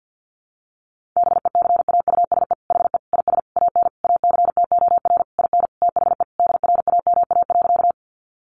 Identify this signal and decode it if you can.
Morse code, heard as a keyed tone, about 50 wpm, 700 Hz. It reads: BECAUSE HE IS AN ACTOR IN THE DRAMA.
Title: Sentences from Top 1500 Words 50wpm